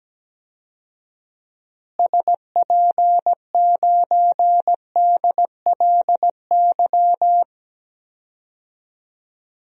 Morse code, heard as a keyed tone, about 17 wpm, 700 Hz. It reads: SP9DLY
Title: DXCC-100 Call Signs - 6 Characters - Repeated 3x - 17wpm